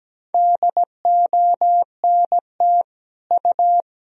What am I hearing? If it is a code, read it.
Morse code, heard as a keyed tone, about 17 wpm, 700 Hz. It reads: DONT U
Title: Sentences from Top 300 Words and Abbreviations 17wpm